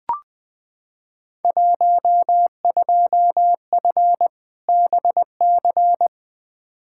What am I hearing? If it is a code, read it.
Morse code, heard as a keyed tone, about 20 wpm, 700 Hz. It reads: 12F BC
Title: Field Day Exchanges Encore - 20wpm